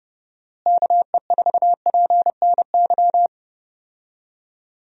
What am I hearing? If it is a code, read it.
Morse code, heard as a keyed tone, about 30 wpm, 700 Hz. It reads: KE4PNY
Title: USA Call Signs - 6 Characters - Repeated 3x - 30wpm